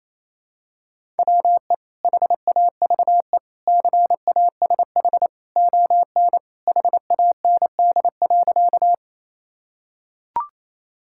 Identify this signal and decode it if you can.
Morse code, heard as a keyed tone, about 27 wpm, 700 Hz. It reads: WE HAVE CASH ON HAND.
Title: Sentences from Top 1000 Words - Review Part 5 - 28wpm